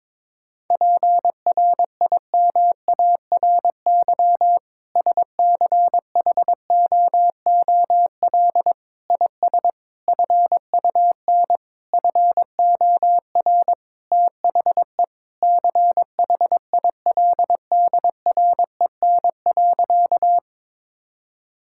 Morse code, heard as a keyed tone, about 22 wpm, 700 Hz. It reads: PRIMARY SCHOOL IS FUN FOR THE CHILDREN.
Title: Sentences from Top 1000 Words - Review Part 5 - 22wpm